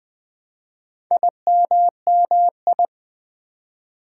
Morse code, heard as a keyed tone, about 20 wpm, 700 Hz. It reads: IMMI
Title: Build A Word - 4 Letters At A Time - 11-Letter Words - 20wpm